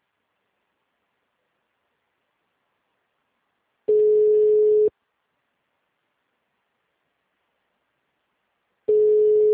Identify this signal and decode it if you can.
Ringback tone (Europe, 425 Hz).